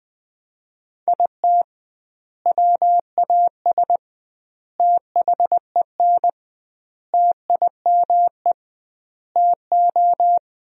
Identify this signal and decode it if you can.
Morse code, heard as a keyed tone, about 20 wpm, 700 Hz. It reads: IT WAS THEN TIME TO
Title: 2x Word-Spacing - Sentences from Top 300 Words - 20wpm